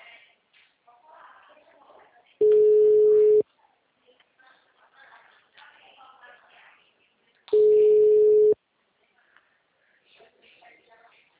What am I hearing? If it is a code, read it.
Ringback tone (Europe, 425 Hz), with speech.